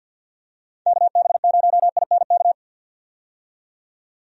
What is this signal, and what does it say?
Morse code, heard as a keyed tone, about 46 wpm, 700 Hz. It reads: KB0INK
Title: USA Call Signs - 6 Characters - 50wpm